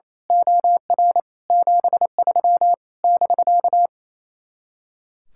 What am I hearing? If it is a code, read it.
Morse code, heard as a keyed tone, about 26 wpm, 700 Hz. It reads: OR 73 <BK>
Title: Speed-Racing Common QSO Phrases - 28wpm